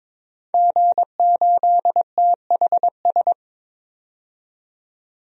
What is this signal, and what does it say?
Morse code, heard as a keyed tone, about 22 wpm, 700 Hz. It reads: G8THS
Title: DXCC-100 Call Signs - 5 Characters - 22wpm